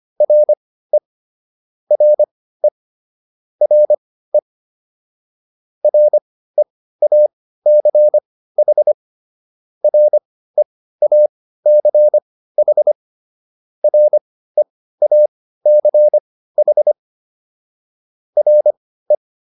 Morse code, heard as a keyed tone, about 25 wpm, spaced as for 15 wpm, 600 Hz. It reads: RE RE RE REACH REACH REACH RE